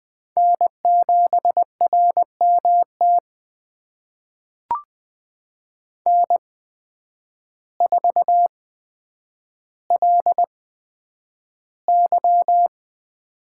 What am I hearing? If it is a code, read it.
Morse code, heard as a keyed tone, about 20 wpm, spaced as for 9 wpm, 700 Hz. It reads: N7RMT N4LY
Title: USA Call Signs - 5 Characters - 8x Character Spacing - 20wpm